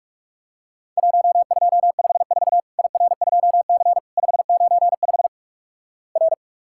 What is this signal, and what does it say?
Morse code, heard as a keyed tone, about 46 wpm, 700 Hz. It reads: 1254 IR2C 595 R
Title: Worked All Europe DX Contest - QTC Exchange - 45wpm